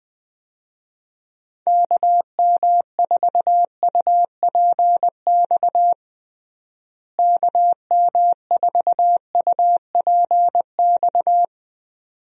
Morse code, heard as a keyed tone, about 20 wpm, 700 Hz. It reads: KM4UPX KM4UPX
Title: USA Call Signs - 6 Characters - Repeated 2x - 20wpm